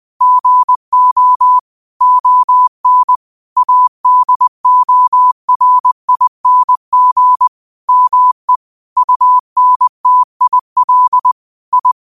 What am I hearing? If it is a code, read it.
Morse code, heard as a keyed tone, about 20 wpm, 1000 Hz. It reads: GO ON ADORING ME UNTIL I
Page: Quotes for Fri, 15 Aug 2025 in Morse Code at 20 words per minute.